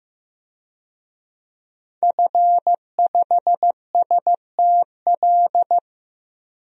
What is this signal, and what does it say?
Morse code, heard as a keyed tone, about 15 wpm, 700 Hz. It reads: F5STL
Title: DXCC-100 Call Signs - 5 Characters - Repeated 3x - 15wpm